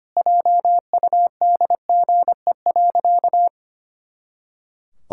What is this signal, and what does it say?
Morse code, heard as a keyed tone, about 25 wpm, 700 Hz. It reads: JUDGE.
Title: Pangram Challenge v2 - 25wpm